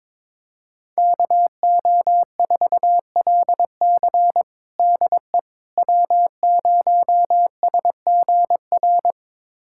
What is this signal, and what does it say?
Morse code, heard as a keyed tone, about 22 wpm, 700 Hz. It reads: KO4LC DE W0SGR
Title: USA Callsign DE Callsign - 5 Characters - 22wpm